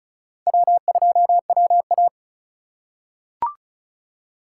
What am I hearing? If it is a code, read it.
Morse code, heard as a keyed tone, about 34 wpm, 700 Hz. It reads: W2WA
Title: USA Call Signs - 4 Characters - 4x Character Spacing - 35wpm